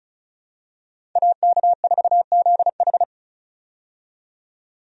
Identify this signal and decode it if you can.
Morse code, heard as a keyed tone, about 36 wpm, 700 Hz. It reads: AK4ZH
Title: USA Call Signs - 5 Characters - 35wpm